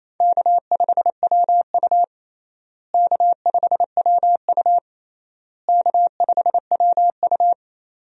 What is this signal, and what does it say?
Morse code, heard as a keyed tone, about 28 wpm, 700 Hz. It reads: K5WU K5WU K5WU
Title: USA Call Signs - 4 Characters - Repeated 3x - 28wpm